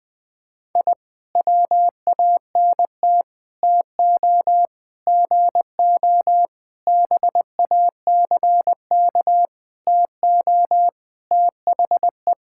Morse code, heard as a keyed tone, about 20 wpm, 700 Hz. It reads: I WANT TO GO BACK TO THE
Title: Sentences with 5-Word Phrases - 20wpm